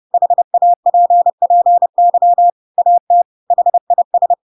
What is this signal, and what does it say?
Morse code, heard as a keyed tone, about 30 wpm, 700 Hz. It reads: HAPPY AT HIS
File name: Pinocchio / Pinocchio ch4 30 30 ik1hge